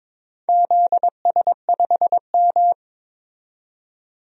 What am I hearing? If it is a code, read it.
Morse code, heard as a keyed tone, about 22 wpm, 700 Hz. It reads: ZS5M
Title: DXCC-100 Call Signs - 4 Characters - 22wpm